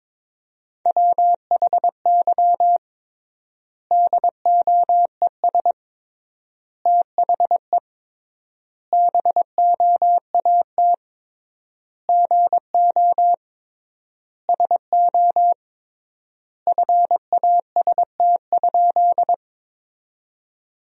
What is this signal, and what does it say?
Morse code, heard as a keyed tone, about 22 wpm, 700 Hz. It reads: WHY DOES THE BOAT GO SO FAST?
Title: 3x Word-Spacing - Sentences from Top 500 Words - 22wpm